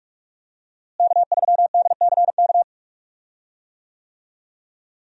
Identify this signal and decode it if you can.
Morse code, heard as a keyed tone, about 46 wpm, 700 Hz. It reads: K3DCK
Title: USA Call Signs - 5 Characters - Repeated 2x - 45wpm